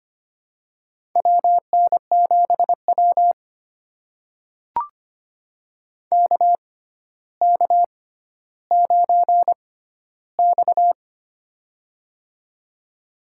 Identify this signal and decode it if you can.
Morse code, heard as a keyed tone, about 25 wpm, spaced as for 13 wpm, 700 Hz. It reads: WN7W KK9X
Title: USA Call Signs - 4 Characters - 6x Character Spacing - 25wpm